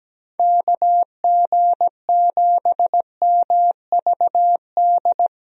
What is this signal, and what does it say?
Morse code, heard as a keyed tone, about 17 wpm, 700 Hz. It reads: KG7MVD
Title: USA Call Signs - 6 Characters - Repeated 2x - 17wpm